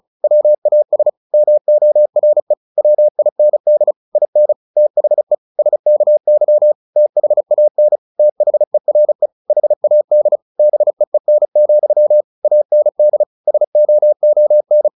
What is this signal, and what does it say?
Morse code, heard as a keyed tone, about 34 wpm, 600 Hz. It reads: WAS MORE WIND IN THE SKY THAN THERE HAD BEEN, AND SOON
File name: the old man and the sea 2 35wpm